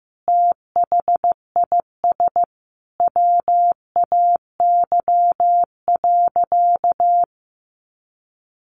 Morse code, heard as a keyed tone, about 15 wpm, 700 Hz. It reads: THIS WAY.
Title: Sentences from Top 100 Words 15wpm